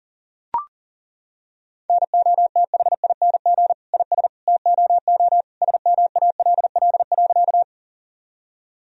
Morse code, heard as a keyed tone, about 40 wpm, 700 Hz. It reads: NOTHING IS TOO SMALL.
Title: Sentences from Top 1000 Words - Review Part 1 - 40wpm